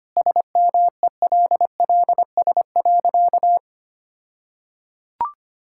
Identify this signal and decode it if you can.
Morse code, heard as a keyed tone, about 25 wpm, 700 Hz. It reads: SMELLS.
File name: practice / 25wpm / All-Age Appropriate Jokes - 25wpm